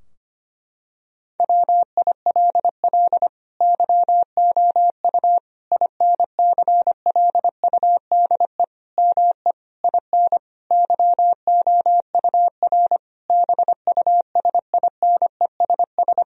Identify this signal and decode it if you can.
Morse code, heard as a keyed tone, about 25 wpm, 700 Hz. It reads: WILL YOU INCLUDE ME IN YOUR BUSINESS
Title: Sentences from Top 1000 Words - Review Part 3 - 25wpm